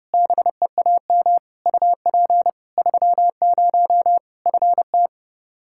Morse code, heard as a keyed tone, about 30 wpm, 700 Hz. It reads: BEAM UP 30 FT
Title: Practice QSO - Exchange 2 - Info - 30wpm